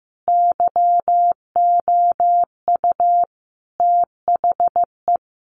Morse code, heard as a keyed tone, about 15 wpm, 700 Hz. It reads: YOU THE
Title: A Sound Of Thunder 15wpm